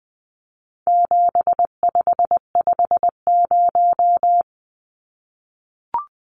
Morse code, heard as a keyed tone, about 20 wpm, 700 Hz. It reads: 7550
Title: Numbers - 4-Digit 20wpm